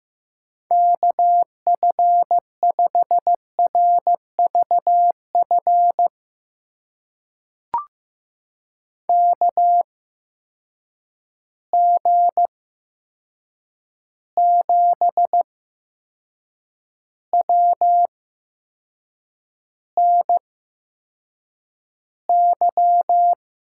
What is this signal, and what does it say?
Morse code, heard as a keyed tone, about 15 wpm, spaced as for 6 wpm, 700 Hz. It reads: KF5RVF KG7WNY
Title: USA Call Signs - 6 Characters - 8x Character Spacing - 15wpm